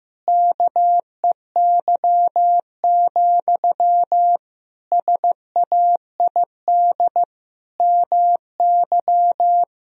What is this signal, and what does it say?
Morse code, heard as a keyed tone, about 16 wpm, 700 Hz. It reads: KEY, SAID MY
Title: Sentences from Top 1000 Words - Review Part 4 - 15wpm